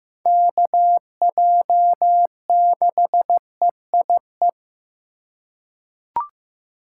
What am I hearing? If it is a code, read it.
Morse code, heard as a keyed tone, about 15 wpm, 700 Hz. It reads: KJ6EIE
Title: USA Call Signs - 6 Characters - 2x Character Spacing - 15wpm